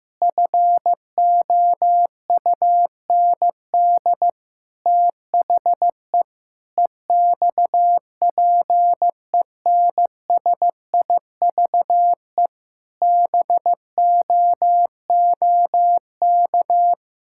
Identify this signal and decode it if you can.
Morse code, heard as a keyed tone, about 15 wpm, 700 Hz. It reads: FOUND THE EXPENSIVE BOOK
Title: Sentences from Top 1000 Words - Review Part 6 - 15wpm